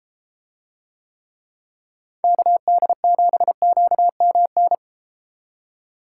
Morse code, heard as a keyed tone, about 32 wpm, 700 Hz. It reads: KD7QMN